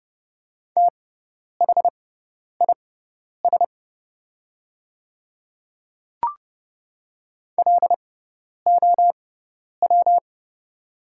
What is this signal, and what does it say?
Morse code, heard as a keyed tone, about 31 wpm, spaced as for 10 wpm, 700 Hz. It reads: THIS LOW
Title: 6x Character Spacing - Top 300 Words - 30wpm